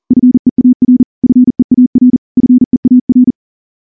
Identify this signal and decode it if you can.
Morse code, heard as a keyed tone, about 38 wpm, 250 Hz. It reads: FEAR FEAR FEAR
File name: morse-training-40WPM